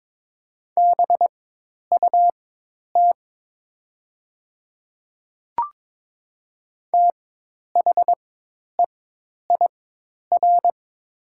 Morse code, heard as a keyed tone, about 22 wpm, spaced as for 10 wpm, 700 Hz. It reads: BUT THEIR
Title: 4x Character Spacing - Top 100 Words - 22wpm